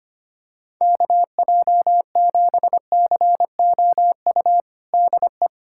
Morse code, heard as a keyed tone, about 25 wpm, 700 Hz. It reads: KJ7COU DE
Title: USA Callsign DE Callsign 25wpm